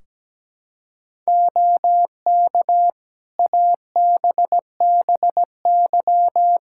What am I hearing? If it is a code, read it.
Morse code, heard as a keyed tone, about 17 wpm, 700 Hz. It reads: OK ABBY
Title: Practice QSO - Exchange 3 - More Info - 17wpm